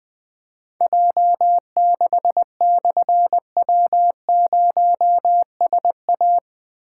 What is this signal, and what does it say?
Morse code, heard as a keyed tone, about 20 wpm, 700 Hz. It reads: J6/W0SA
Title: Difficult Call Signs 20wpm